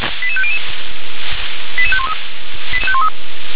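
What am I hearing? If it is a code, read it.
Five-tone selective call (ZVEI), heard as quick runs of five tones, tones of about 70 ms. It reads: 88500 88513
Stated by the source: ZVEI five-tone selective call received on SDR.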